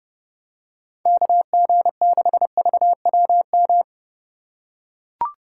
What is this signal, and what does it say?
Morse code, heard as a keyed tone, about 30 wpm, 700 Hz. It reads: KG6VWM
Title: USA Call Signs - 6 Characters - 2x Character Spacing - 30wpm